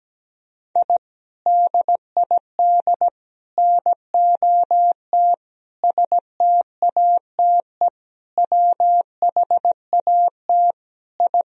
Morse code, heard as a keyed tone, about 17 wpm, 700 Hz. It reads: I DID NOT STATE WHAT I
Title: Sentences from Top 1000 Words - Review Part 2 - 17wpm